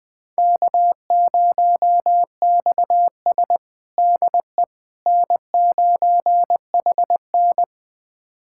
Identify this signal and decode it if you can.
Morse code, heard as a keyed tone, about 20 wpm, 700 Hz. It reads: K0XS DE N9HN
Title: USA Callsign DE Callsign - 4 Characters - 20wpm